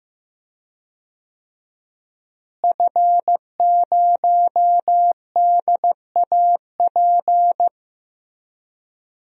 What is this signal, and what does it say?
Morse code, heard as a keyed tone, about 15 wpm, 700 Hz. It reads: F0DAP
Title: DXCC-100 Call Signs - 5 Characters - Repeated 2x - 15wpm